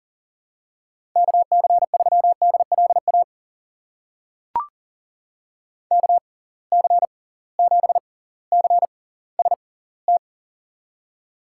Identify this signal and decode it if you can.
Morse code, heard as a keyed tone, about 37 wpm, spaced as for 20 wpm, 700 Hz. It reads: KC3DLA KC7CST
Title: USA Call Signs - 6 Characters - 6x Character Spacing - 40wpm